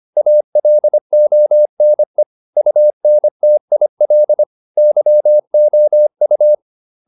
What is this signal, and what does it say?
Morse code, heard as a keyed tone, about 25 wpm, 600 Hz. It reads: ALONE UNTIL YOU
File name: the old man and the sea part 1 25wpm